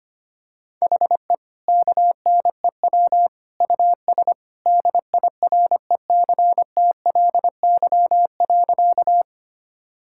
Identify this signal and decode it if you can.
Morse code, heard as a keyed tone, about 25 wpm, 700 Hz. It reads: HE KNEW US DIRECTLY.
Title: Sentences from Top 1000 Words - Review Part 5 - 25wpm